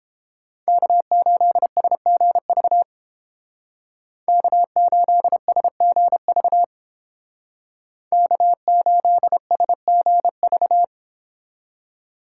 Morse code, heard as a keyed tone, about 30 wpm, 700 Hz. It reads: K8SGV K8SGV K8SGV